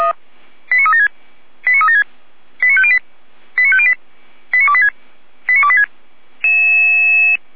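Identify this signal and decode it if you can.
Five-tone selective call (ZVEI), heard as quick runs of five tones, tones of about 70 ms. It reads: 89376 89376 89508 89508 89286 89286
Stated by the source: ZVEI five-tone call received by SDR, from the audio server of Feuerwehr Orlamünde (f85360).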